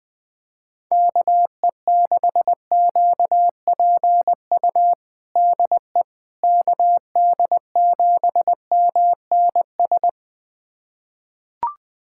Morse code, heard as a keyed tone, about 20 wpm, 700 Hz. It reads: KE6QPU DE KD7MNS
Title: USA Callsign DE Callsign 20wpm